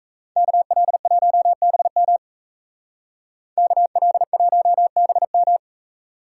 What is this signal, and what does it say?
Morse code, heard as a keyed tone, about 38 wpm, 700 Hz. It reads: KL1BM KL1BM